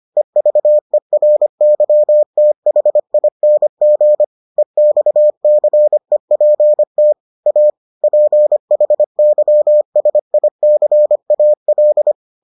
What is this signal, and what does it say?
Morse code, heard as a keyed tone, about 25 wpm, 600 Hz. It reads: EVERYTHING EXCEPT A PHYSICAL